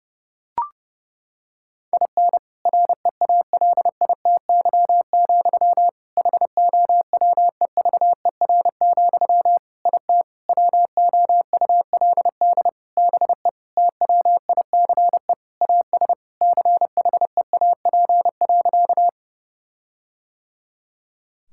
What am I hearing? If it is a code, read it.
Morse code, heard as a keyed tone, about 30 wpm, 700 Hz. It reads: IN REALITY, HOWEVER, IT WOULD BE TWICE AS CHEAP.
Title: Sentences from Top 1600 Words 30wpm